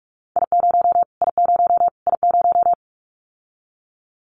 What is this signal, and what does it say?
Morse code, heard as a keyed tone, about 44 wpm, 700 Hz. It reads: I0 I0 I0
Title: Callsign Prefixes for Top 25 DXCC - Repeated 3x - 45wpm